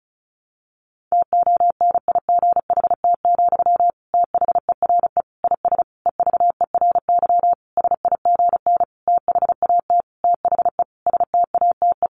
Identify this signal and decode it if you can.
Morse code, heard as a keyed tone, about 34 wpm, 700 Hz. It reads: TONIGHT, THERE IS EVERY SIGN THAT THE STATE
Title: Sentences from Top 1400 Words 35wpm